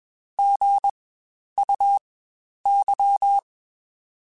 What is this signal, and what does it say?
Morse code, heard as a keyed tone, about 21 wpm, spaced as for 10 wpm, 800 Hz. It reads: GUY
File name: Spaced Alphas